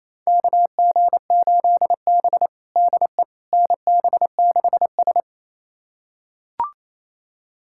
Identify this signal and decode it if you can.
Morse code, heard as a keyed tone, about 28 wpm, 700 Hz. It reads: KG8B DE NB6S
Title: USA Callsign DE Callsign - 4 Characters - 28wpm